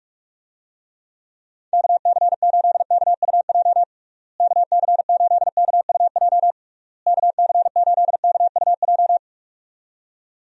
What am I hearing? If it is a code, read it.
Morse code, heard as a keyed tone, about 47 wpm, 700 Hz. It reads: KC8KUJ KC8KUJ KC8KUJ
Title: USA Call Signs - 6 Characters - Repeated 3x - 45wpm